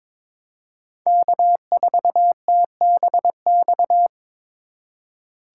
Morse code, heard as a keyed tone, about 21 wpm, 700 Hz. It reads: K4TBX
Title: USA Call Signs - 5 Characters - 4x Character Spacing - 22wpm